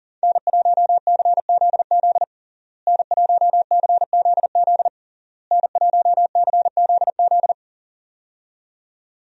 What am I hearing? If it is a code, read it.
Morse code, heard as a keyed tone, about 40 wpm, 700 Hz. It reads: N1CZZ N1CZZ N1CZZ
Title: USA Call Signs - 5 Characters - Repeated 3x - 40wpm